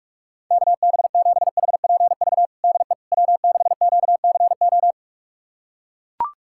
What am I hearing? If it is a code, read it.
Morse code, heard as a keyed tone, about 42 wpm, 700 Hz. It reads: KB7HPV DE W6QCO
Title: USA Callsign DE Callsign 45wpm